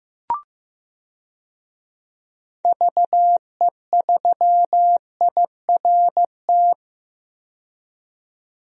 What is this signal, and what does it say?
Morse code, heard as a keyed tone, about 15 wpm, 700 Hz. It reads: VE3IRT
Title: DXCC-100 Call Signs - 6 Characters - 15wpm